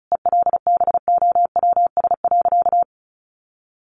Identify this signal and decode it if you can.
Morse code, heard as a keyed tone, about 34 wpm, 700 Hz. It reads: ELBOWS.
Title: A Sound Of Thunder 35wpm